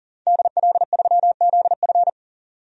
Morse code, heard as a keyed tone, about 40 wpm, 700 Hz. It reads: DL3ZF
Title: DXCC-100 Call Signs - 5 Characters - 40wpm